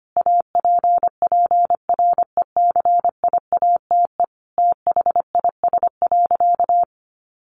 Morse code, heard as a keyed tone, about 25 wpm, 700 Hz. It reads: APPRECIATE THIS.
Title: Sentences from Top 1300 Words 25wpm